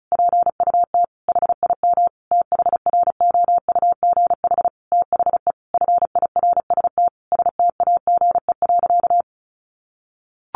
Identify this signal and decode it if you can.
Morse code, heard as a keyed tone, about 34 wpm, 700 Hz. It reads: PUT HIM THROUGH THE FIRST STAGE.
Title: Sentences from Top 1200 Words 35wpm